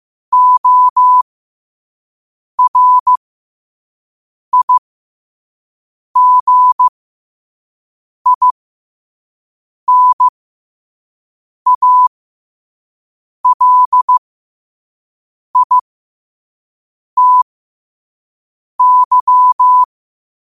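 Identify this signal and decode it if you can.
Morse code, heard as a keyed tone, about 15 wpm, spaced as for 5 wpm, 1000 Hz. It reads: ORIGINALITY
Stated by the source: Morse code sent at 5 wpm for learning.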